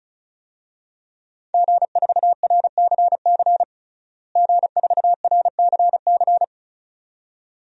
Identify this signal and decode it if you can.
Morse code, heard as a keyed tone, about 34 wpm, 700 Hz. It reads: G4RCC G4RCC
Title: DXCC-100 Call Signs - 5 Characters - Repeated 2x - 35wpm